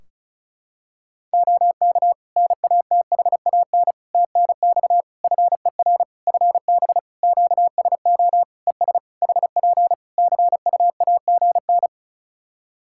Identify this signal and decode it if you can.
Morse code, heard as a keyed tone, about 36 wpm, 700 Hz. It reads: OK NATHAN TNX FER FB QSO ES HP CUAGN
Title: Practice QSO - Exchange 4 - Ending - 35wpm